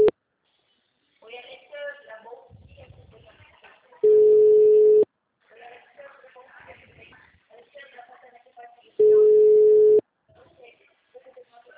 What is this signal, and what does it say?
Ringback tone (Europe, 425 Hz), with speech.
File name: out- -2012- - - .1898